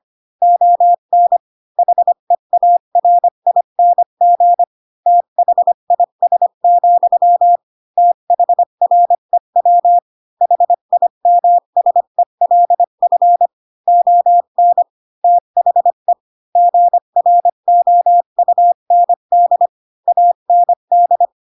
Morse code, heard as a keyed tone, about 25 wpm, 700 Hz. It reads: ON HEARING THIS, THREW HIMSELF ON THE GROUND AND
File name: Pinocchio / Pinocchio ch3 20 25 ik1hge